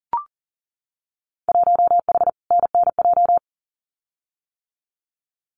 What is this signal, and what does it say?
Morse code, heard as a keyed tone, about 40 wpm, 700 Hz. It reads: 1H NNJ
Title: Winter Field Day Exchanges 40wpm